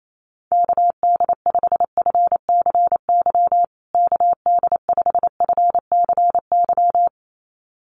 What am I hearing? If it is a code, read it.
Morse code, heard as a keyed tone, about 28 wpm, 700 Hz. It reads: KD5FCY KD5FCY
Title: USA Call Signs Repeated 2x 28wpm